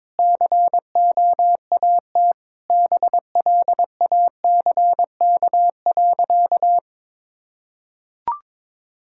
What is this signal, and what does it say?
Morse code, heard as a keyed tone, about 22 wpm, 700 Hz. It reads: COAT BLACK.
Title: Sentences from Top 1600 Words 22wpm